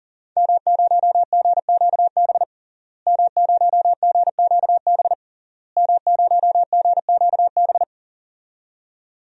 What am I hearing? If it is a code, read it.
Morse code, heard as a keyed tone, about 40 wpm, 700 Hz. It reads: M0GQB M0GQB M0GQB
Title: DXCC-100 Call Signs - 5 Characters - Repeated 3x - 40wpm